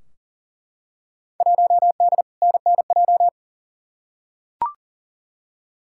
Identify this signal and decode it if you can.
Morse code, heard as a keyed tone, about 40 wpm, 700 Hz. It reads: 1D NNJ
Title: Field Day Exchanges - 40wpm